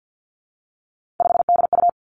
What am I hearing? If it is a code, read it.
Morse code, heard as a keyed tone, about 50 wpm, 700 Hz. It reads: 5DU
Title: Three Challenging Letter-Number Combinations 50wpm